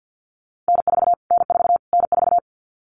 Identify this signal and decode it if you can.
Morse code, heard as a keyed tone, about 50 wpm, 700 Hz. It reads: N4 N4 N4
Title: Callsign Prefixes for Top 25 DXCC - Repeated 3x - 50wpm